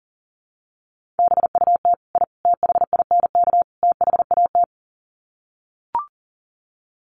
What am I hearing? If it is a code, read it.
Morse code, heard as a keyed tone, about 40 wpm, 700 Hz. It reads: BUT I THINK THAT
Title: Sets of 4 Words 40wpm